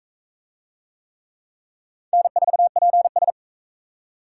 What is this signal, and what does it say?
Morse code, heard as a keyed tone, about 39 wpm, 700 Hz. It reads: N4PS